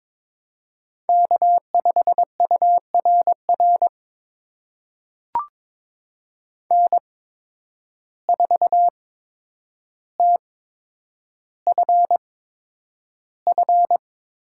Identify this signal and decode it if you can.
Morse code, heard as a keyed tone, about 22 wpm, spaced as for 9 wpm, 700 Hz. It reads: K5URR N4TFF
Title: USA Call Signs - 5 Characters - 8x Character Spacing - 22wpm